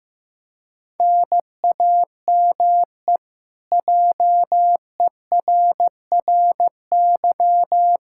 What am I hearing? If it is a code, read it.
Morse code, heard as a keyed tone, about 15 wpm, 700 Hz. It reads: NAME JERRY
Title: Practice QSO - Exchange 1 - Beginning - 15wpm